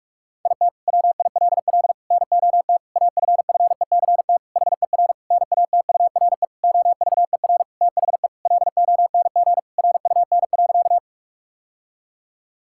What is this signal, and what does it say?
Morse code, heard as a keyed tone, about 41 wpm, 700 Hz. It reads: IT WILL NOT AFFECT HER NATURE OVER THE LONG RUN.
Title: Sentences from Top 1000 Words - Review Part 6 - 45wpm